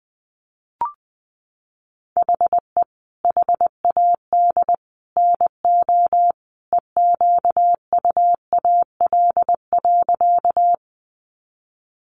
Morse code, heard as a keyed tone, about 20 wpm, 700 Hz. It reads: HE HAD NO EQUAL.
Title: Sentences from Top 1400 Words 20wpm